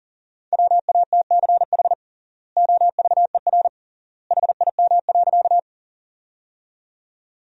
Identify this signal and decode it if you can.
Morse code, heard as a keyed tone, about 40 wpm, 700 Hz. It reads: WATCH OVER HIM.
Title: 3x Word-Spacing - Sentences from Top 300 Words - 40wpm